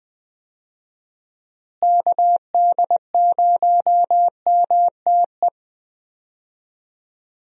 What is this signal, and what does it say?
Morse code, heard as a keyed tone, about 20 wpm, 700 Hz. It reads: KD0MTE
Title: USA Call Signs - 6 Characters - Repeated 2x - 20wpm